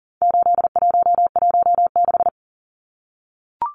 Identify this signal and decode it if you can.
Morse code, heard as a keyed tone, about 40 wpm, 700 Hz. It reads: 8116
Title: Numbers - 4-Digit 40wpm